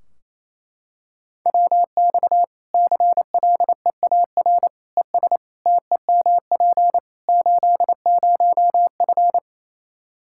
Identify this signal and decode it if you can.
Morse code, heard as a keyed tone, about 27 wpm, 700 Hz. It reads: WX CLEAR ES TEMP 80F
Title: K1USN SST QSO Practice 28wpm